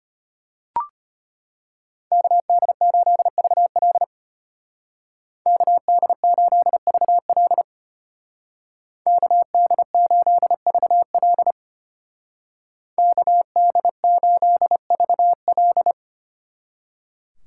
Morse code, heard as a keyed tone, about 33 wpm, 700 Hz. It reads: KD8VL KD8VL KD8VL KD8VL